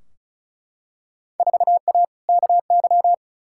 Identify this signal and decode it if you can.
Morse code, heard as a keyed tone, about 36 wpm, 700 Hz. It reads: 4A KY
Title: Field Day Exchanges Encore - 35wpm